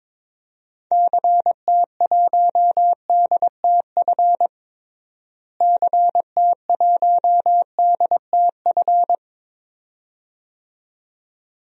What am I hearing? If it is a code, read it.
Morse code, heard as a keyed tone, about 22 wpm, 700 Hz. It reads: CT1DTF CT1DTF
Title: DXCC-100 Call Signs - 6 Characters - Repeated 2x - 22wpm